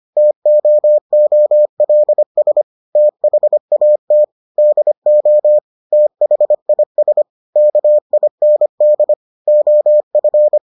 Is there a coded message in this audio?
Yes: Morse code from 0.2 to 10.6 s, heard as a keyed tone, about 25 wpm, 600 Hz. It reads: TOOLS THAT DO THIS KIND OF